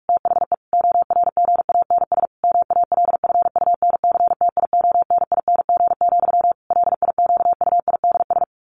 Morse code, heard as a keyed tone, about 46 wpm, 700 Hz. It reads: THE ORGANS MALFUNCTIONING, LIQUIDS
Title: A Sound Of Thunder 45wpm